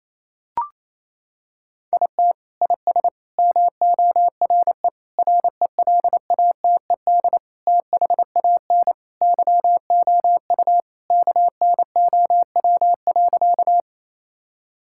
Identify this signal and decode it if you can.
Morse code, heard as a keyed tone, about 28 wpm, 700 Hz. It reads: IT IS MORE RELATED THAN YOU KNOW.
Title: Sentences from Top 1000 Words - Review Part 5 - 28wpm